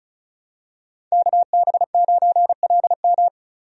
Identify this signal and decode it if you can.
Morse code, heard as a keyed tone, about 34 wpm, 700 Hz. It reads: KB9LM
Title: USA Call Signs - 5 Characters - 35wpm